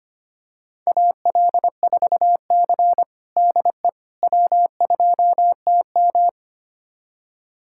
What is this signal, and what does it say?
Morse code, heard as a keyed tone, about 25 wpm, 700 Hz. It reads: AL4C DE W2TM
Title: USA Callsign DE Callsign - 4 Characters - 25wpm